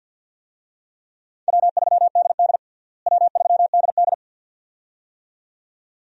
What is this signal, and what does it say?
Morse code, heard as a keyed tone, about 48 wpm, 700 Hz. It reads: W3DD W3DD
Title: USA Call Signs - 4 Characters - Repeated 2x - 50wpm